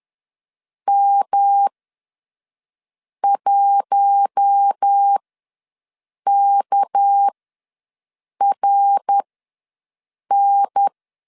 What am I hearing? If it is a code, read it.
Morse code, heard as a keyed tone, about 10 wpm, spaced as for 5 wpm, 800 Hz. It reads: M1KRN
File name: LEZIONI / 6 PKX0 / 6 PKX0 ws 10wpm 4220